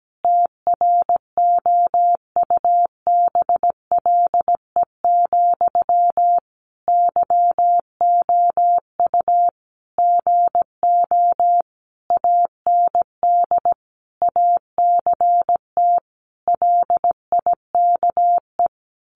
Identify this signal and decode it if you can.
Morse code, heard as a keyed tone, about 17 wpm, 700 Hz. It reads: TROUBLE, YOU GO AND ACT LIKE
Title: Sentences from Top 1200 Words 17wpm